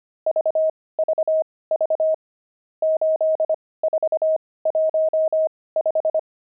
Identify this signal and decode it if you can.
Morse code, heard as a keyed tone, about 25 wpm, spaced as for 18 wpm, 630 Hz. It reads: VVV 8415
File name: pangram-fs-18-cs-25-nums4